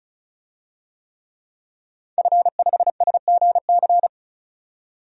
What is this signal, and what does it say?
Morse code, heard as a keyed tone, about 34 wpm, 700 Hz. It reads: F5SGC
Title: DXCC-100 Call Signs - 5 Characters - 35wpm